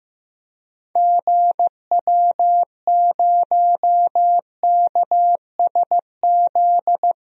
Morse code, heard as a keyed tone, about 15 wpm, 700 Hz. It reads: GW0KSZ
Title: DXCC-100 Call Signs - 6 Characters - Repeated 3x - 15wpm